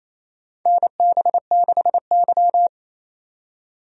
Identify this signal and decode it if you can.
Morse code, heard as a keyed tone, about 28 wpm, 700 Hz. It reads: NB6Y
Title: USA Call Signs - 4 Characters - 28wpm